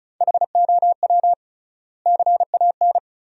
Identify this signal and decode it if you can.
Morse code, heard as a keyed tone, about 36 wpm, 700 Hz. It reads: HOW CAN
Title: 3x Word-Spacing - Sentences from Top 200 Words - 35wpm